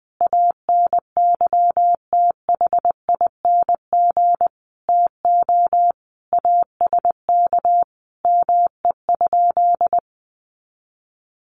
Morse code, heard as a keyed tone, about 19 wpm, 700 Hz. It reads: ANYTHING TO ASK ME?
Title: Sentences from Top 1000 Words 20wpm